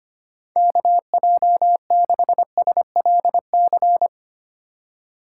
Morse code, heard as a keyed tone, about 32 wpm, 700 Hz. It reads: KJ6SLC